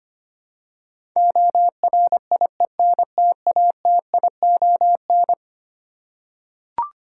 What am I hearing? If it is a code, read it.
Morse code, heard as a keyed tone, about 25 wpm, 700 Hz. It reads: ORIENTATION
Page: morse-code-ninja-practice-25wpm